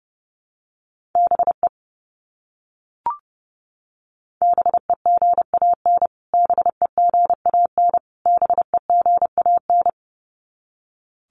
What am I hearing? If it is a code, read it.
Morse code, heard as a keyed tone, about 30 wpm, 700 Hz. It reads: BE BEGAN BEGAN BEGAN
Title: Top 500 Words Repeated 30wpm